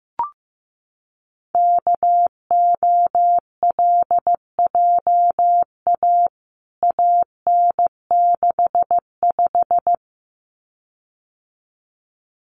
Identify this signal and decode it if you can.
Morse code, heard as a keyed tone, about 15 wpm, 700 Hz. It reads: KOLJA AN65
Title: CWT Exchange 15wpm